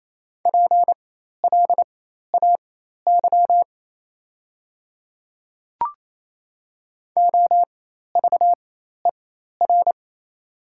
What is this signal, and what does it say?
Morse code, heard as a keyed tone, about 27 wpm, spaced as for 13 wpm, 690 Hz. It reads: PLAY OVER
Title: 4x Character Spacing - Top 300 Words - 28wpm